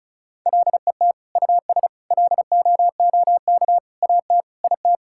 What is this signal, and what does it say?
Morse code, heard as a keyed tone, about 34 wpm, 700 Hz. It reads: LET US LOOK AT IT
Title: Sentences from Top 1000 Words - Review Part 6 - 35wpm